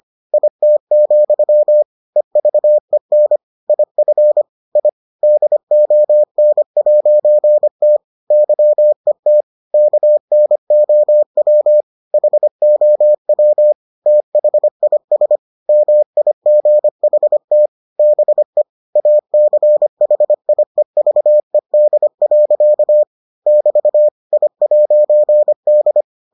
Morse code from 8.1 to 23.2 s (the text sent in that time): YET KNOW HOW THIS MIGHT BE ACHIEVED.